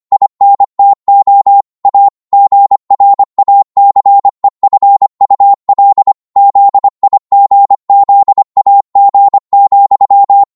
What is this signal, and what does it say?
Morse code, heard as a keyed tone, about 25 wpm, 800 Hz. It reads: INTO A GRACEFUL ZIGZAG,
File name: Morse / psg-25wpm